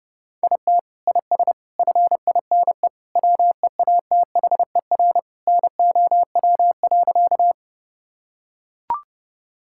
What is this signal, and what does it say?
Morse code, heard as a keyed tone, about 30 wpm, 700 Hz. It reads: IT IS FINE WEATHER NOW.
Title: Sentences from Top 1000 Words - Review Part 5 - 30wpm